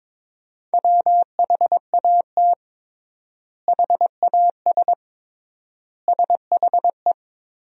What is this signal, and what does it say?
Morse code, heard as a keyed tone, about 22 wpm, 700 Hz. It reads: WHAT HAS SHE
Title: 3x Word-Spacing - Sentences from Top 100 Words - 22wpm